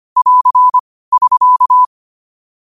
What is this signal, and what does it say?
Morse code, heard as a keyed tone, about 25 wpm, 1000 Hz. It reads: <AR> <SK>
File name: qotdfiles / 25wpm